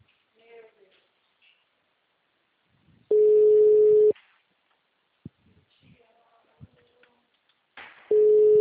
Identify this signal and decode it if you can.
Ringback tone (Europe, 425 Hz), with speech.
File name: out- -2002- - - .7739